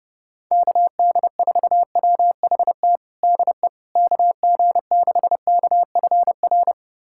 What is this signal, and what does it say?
Morse code, heard as a keyed tone, about 32 wpm, 700 Hz. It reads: KD4WHT DE KG6KFR
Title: USA Callsign DE Callsign - 6 Characters - 30wpm